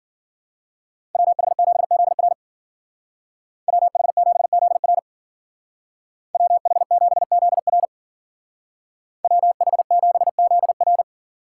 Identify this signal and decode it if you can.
Morse code, heard as a keyed tone, about 50 wpm, 700 Hz. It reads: WH7ZR WH7ZR WH7ZR WH7ZR
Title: Speed Racing - USA 2x2 Call Signs 40wpm